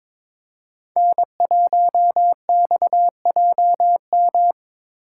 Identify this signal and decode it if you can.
Morse code, heard as a keyed tone, about 22 wpm, 700 Hz. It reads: N1XJM
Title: K1USN SST QSO Practice - 22wpm